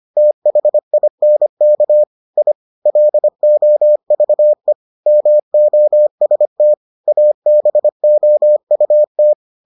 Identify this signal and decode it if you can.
Morse code, heard as a keyed tone, about 25 wpm, 600 Hz. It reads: THINK I LOVE MOST ABOUT